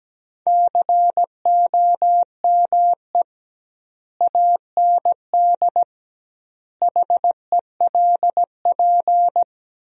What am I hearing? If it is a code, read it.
Morse code, heard as a keyed tone, about 17 wpm, 700 Hz. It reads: COME AND HELP
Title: 2x Word-Spacing - Sentences from Top 200 Words - 17wpm